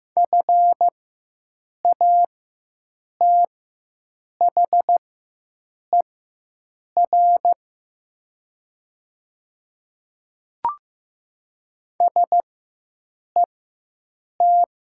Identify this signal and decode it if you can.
Morse code, heard as a keyed tone, about 15 wpm, spaced as for 7 wpm, 700 Hz. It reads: FATHER SET
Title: 4x Character Spacing - Top 200 Words - 15wpm